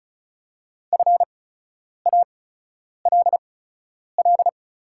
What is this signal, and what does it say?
Morse code, heard as a keyed tone, about 36 wpm, spaced as for 10 wpm, 690 Hz. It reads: FALL
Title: 8x Character Spacing - Top 500 Words - 35wpm